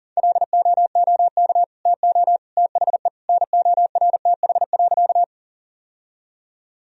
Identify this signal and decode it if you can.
Morse code, heard as a keyed tone, about 40 wpm, 700 Hz. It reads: LOOK TO THE NORTH.
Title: Sentences from Top 1000 Words - Review Part 1 - 40wpm